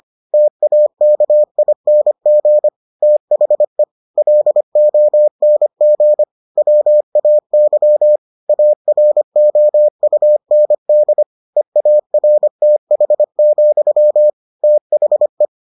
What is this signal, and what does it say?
Morse code, heard as a keyed tone, about 25 wpm, 600 Hz. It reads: TAKING THE LONG WAY AROUND EARTH, THE